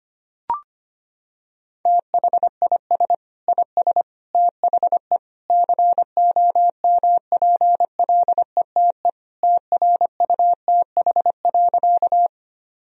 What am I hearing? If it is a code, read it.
Morse code, heard as a keyed tone, about 25 wpm, 700 Hz. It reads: THIS IS THE COMPLETE TRUTH.
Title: Sentences from Top 1000 Words - Review Part 2 - 25wpm